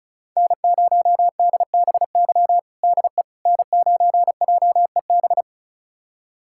Morse code, heard as a keyed tone, about 36 wpm, 700 Hz. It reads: N0DBY DE N9JEB
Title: USA Callsign DE Callsign 35wpm